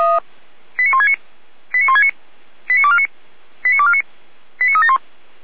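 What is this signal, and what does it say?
Five-tone selective call (ZVEI), heard as quick runs of five tones, tones of about 70 ms. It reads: 89169 89169 89249 89249 89371
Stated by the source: ZVEI five-tone selective call received by SDR.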